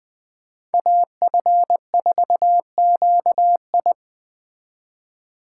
Morse code, heard as a keyed tone, about 20 wpm, 700 Hz. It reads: AF4QI